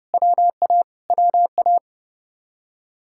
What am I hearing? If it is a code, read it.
Morse code, heard as a keyed tone, about 30 wpm, 700 Hz. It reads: WA WA